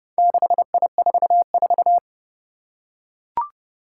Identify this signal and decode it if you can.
Morse code, heard as a keyed tone, about 30 wpm, 700 Hz. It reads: 6I44
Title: Four Challenging Letter-Number Combinations 30wpm